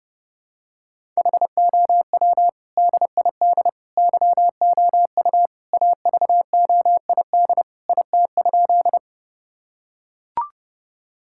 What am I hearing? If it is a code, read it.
Morse code, heard as a keyed tone, about 30 wpm, 700 Hz. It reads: HOW DID YOU AVOID IT?
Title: Sentences from Top 1000 Words - Review Part 6 - 30wpm